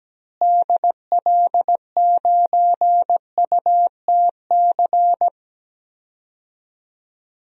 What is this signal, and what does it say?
Morse code, heard as a keyed tone, about 17 wpm, 700 Hz. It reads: DL9UTC
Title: DXCC-100 Call Signs - 6 Characters - Repeated 3x - 17wpm